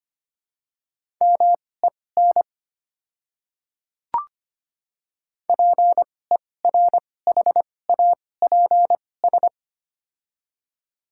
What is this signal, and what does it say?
Morse code, heard as a keyed tone, about 25 wpm, spaced as for 18 wpm, 700 Hz. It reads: MEN PERHAPS
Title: 2x Character Spacing - Top 500 Words - 25wpm